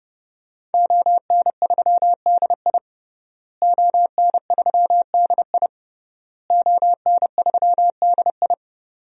Morse code, heard as a keyed tone, about 30 wpm, 700 Hz. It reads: ON3DI ON3DI ON3DI
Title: DXCC-100 Call Signs - 5 Characters - Repeated 3x - 30wpm